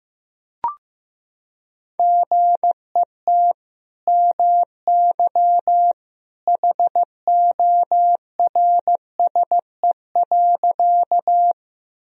Morse code, heard as a keyed tone, about 16 wpm, 700 Hz. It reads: GET MY HORSE.
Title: Sentences from Top 1000 Words - Review Part 1 - 15wpm